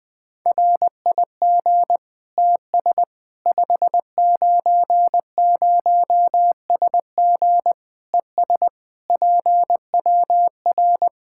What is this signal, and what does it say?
Morse code, heard as a keyed tone, about 20 wpm, 700 Hz. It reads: RIG TS 590SG ES PWR
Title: Practice QSO - Exchange 2 - Info - 20wpm